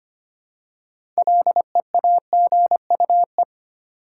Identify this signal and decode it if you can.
Morse code, heard as a keyed tone, about 25 wpm, 700 Hz. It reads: LEAGUE
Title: Six Letter Words - 25wpm